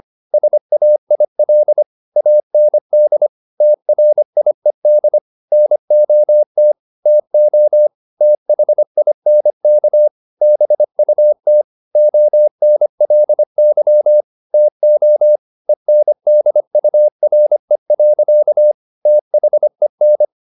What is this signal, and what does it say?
Morse code, heard as a keyed tone, about 25 wpm, 600 Hz. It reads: SAIL AND TRIED NOT TO THINK BUT ONLY TO ENDURE. THEN
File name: the old man and the sea part 1 25wpm